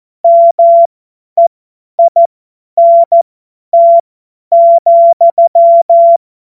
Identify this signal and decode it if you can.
Morse code, heard as a keyed tone, about 14 wpm, spaced as for 10 wpm, 680 Hz. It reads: MEINT,
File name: Witz-08-Juli-2025-50-F70